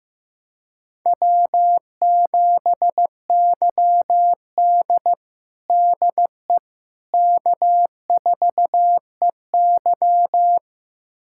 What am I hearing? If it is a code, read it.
Morse code, heard as a keyed tone, about 15 wpm, 700 Hz. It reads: W7YD DE K4EY
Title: USA Callsign DE Callsign - 4 Characters - 15wpm